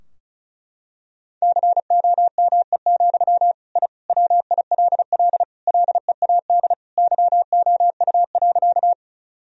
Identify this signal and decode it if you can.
Morse code, heard as a keyed tone, about 36 wpm, 700 Hz. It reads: COME, I WILL LEAD YOU.
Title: Sentences from Top 1000 Words - Review Part 2 - 35wpm